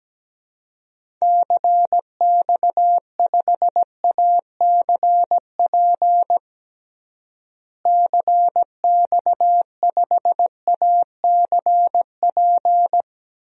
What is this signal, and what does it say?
Morse code, heard as a keyed tone, about 17 wpm, 700 Hz. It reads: CX5ACP CX5ACP
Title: DXCC-100 Call Signs - 6 Characters - Repeated 2x - 17wpm